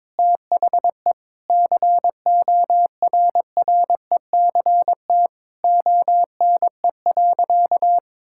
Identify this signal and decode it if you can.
Morse code, heard as a keyed tone, about 22 wpm, 700 Hz. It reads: THE CORRECT ONE.
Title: Sentences from Top 1000 Words - Review Part 6 - 22wpm